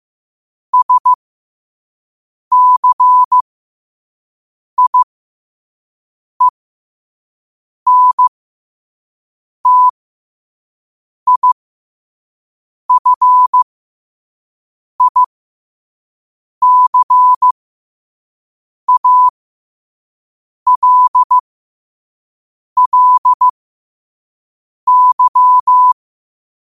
Morse code, heard as a keyed tone, about 15 wpm, spaced as for 5 wpm, 1000 Hz. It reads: SCIENTIFICALLY